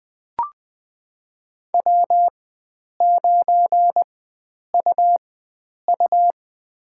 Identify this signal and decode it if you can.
Morse code, heard as a keyed tone, about 20 wpm, spaced as for 9 wpm, 700 Hz. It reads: W9UU
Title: USA Call Signs - 4 Characters - 4x Character Spacing - 20wpm